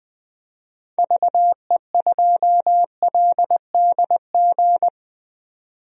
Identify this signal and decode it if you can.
Morse code, heard as a keyed tone, about 20 wpm, 700 Hz. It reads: VE2LDG
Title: DXCC-100 Call Signs - 6 Characters - 20wpm